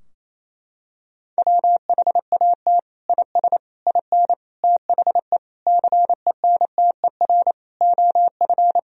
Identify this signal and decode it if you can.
Morse code, heard as a keyed tone, about 27 wpm, 700 Hz. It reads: WHAT IS IN THE CENTER OF
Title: Sentences from Top 1000 Words - Review Part 2 - 28wpm